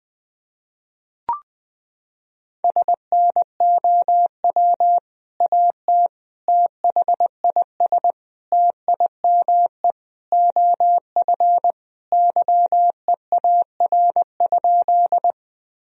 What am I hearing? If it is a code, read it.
Morse code, heard as a keyed tone, about 20 wpm, 700 Hz. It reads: SNOW AT THIS TIME OF YEAR?
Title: Sentences from Top 1000 Words - Review Part 3 - 20wpm